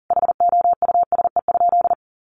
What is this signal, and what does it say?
Morse code, heard as a keyed tone, about 40 wpm, 700 Hz. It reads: HOUSE?
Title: Sentences from Top 1100 Words 40wpm